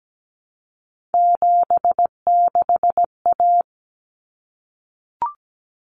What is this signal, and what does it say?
Morse code, heard as a keyed tone, about 17 wpm, 700 Hz. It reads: 76A
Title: Common Cut Numbers - 3-Digit 17wpm